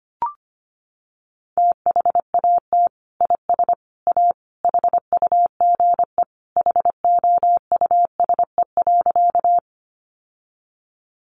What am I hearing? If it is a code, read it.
Morse code, heard as a keyed tone, about 25 wpm, 700 Hz. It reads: THAT IS A HUGE HOUSE.
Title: Sentences from Top 1100 Words 25wpm